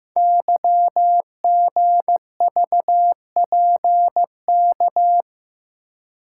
Morse code, heard as a keyed tone, about 15 wpm, 700 Hz. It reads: YGVPK
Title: Code Groups - 5 Letters - 15wpm